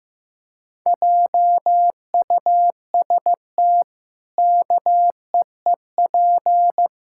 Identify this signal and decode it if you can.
Morse code, heard as a keyed tone, about 15 wpm, 700 Hz. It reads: JUST KEEP
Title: Pangram Challenge v2 - 15wpm